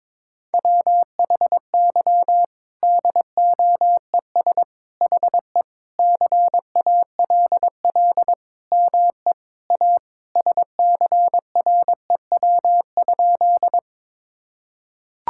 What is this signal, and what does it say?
Morse code, heard as a keyed tone, about 22 wpm, 700 Hz. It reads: WHY DOES HE CALL ME A SCREW?
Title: Sentences from Top 1600 Words 22wpm